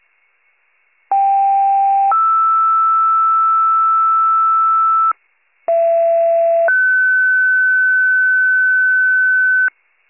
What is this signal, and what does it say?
Two-tone page, heard as a one-second tone, then a three-second tone, about 790 then 1320 Hz, about 670 then 1500 Hz, with noise. It is heard from 1.1 to 9.7 s.